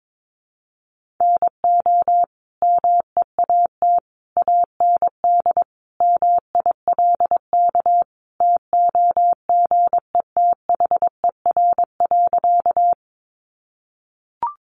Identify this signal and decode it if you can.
Morse code, heard as a keyed tone, about 22 wpm, 700 Hz. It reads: NO MEAT AND MILK TOGETHER.
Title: Sentences from Top 1500 Words 22wpm